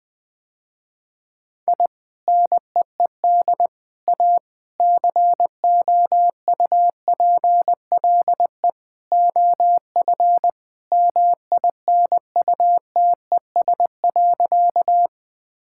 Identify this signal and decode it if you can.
Morse code, heard as a keyed tone, about 20 wpm, 700 Hz. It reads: I NEED A COUPLE OF MINUTES.
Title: Sentences with 3-Word Phrases - Spoken Beginning - 20wpm